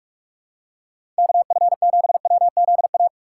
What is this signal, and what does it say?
Morse code, heard as a keyed tone, about 46 wpm, 700 Hz. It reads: KF7WZA
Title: USA Call Signs - 6 Characters - 3x Character Spacing - 45wpm